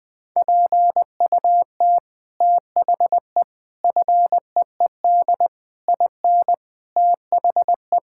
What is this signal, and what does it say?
Morse code, heard as a keyed tone, about 21 wpm, 700 Hz. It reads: PUT THE FEED IN THE
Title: Sentences from Top 1600 Words 20wpm